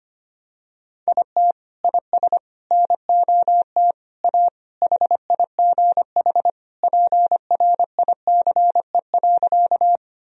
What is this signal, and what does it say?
Morse code, heard as a keyed tone, about 25 wpm, 700 Hz. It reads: IT IS NOT A HIGH PRICE.
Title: Sentences from Top 1000 Words - Review Part 4 - 25wpm